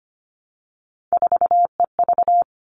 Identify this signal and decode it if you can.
Morse code, heard as a keyed tone, about 25 wpm, 700 Hz. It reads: 4EV
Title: All Cut Numbers - 3-Digit 25wpm